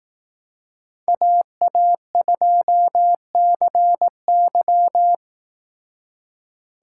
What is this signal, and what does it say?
Morse code, heard as a keyed tone, about 18 wpm, 700 Hz. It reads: AA2CY
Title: Speed Racing - USA 2x2 Call Signs 15wpm